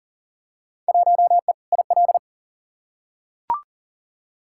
Morse code, heard as a keyed tone, about 40 wpm, 700 Hz. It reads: 1E IL
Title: Field Day Exchanges - 40wpm